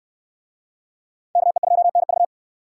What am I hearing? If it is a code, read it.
Morse code, heard as a keyed tone, about 66 wpm, 700 Hz. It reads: KE2NV